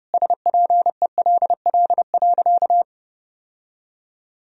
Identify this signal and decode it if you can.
Morse code, heard as a keyed tone, about 30 wpm, 700 Hz. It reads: SPELL.
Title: Sentences from Top 1000 Words - Review Part 1 - 30wpm